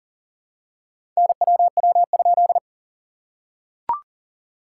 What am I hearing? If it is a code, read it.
Morse code, heard as a keyed tone, about 40 wpm, 690 Hz. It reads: NWW?
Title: SOTA QSO Practice 40wpm